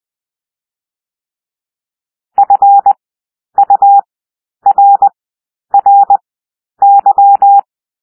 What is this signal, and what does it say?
Morse code, heard as a keyed tone, about 20 wpm, 800 Hz. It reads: F U R R Y
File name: cwmp3 / W1 SOUND WORDS 20 wpm